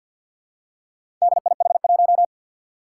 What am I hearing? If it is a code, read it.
Morse code, heard as a keyed tone, about 52 wpm, 700 Hz. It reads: DIH1
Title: Four Challenging Letter-Number Combinations 50wpm